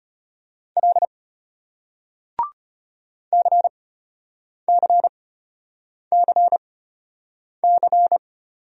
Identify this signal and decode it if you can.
Morse code, heard as a keyed tone, about 34 wpm, spaced as for 8 wpm, 700 Hz. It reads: L CCCC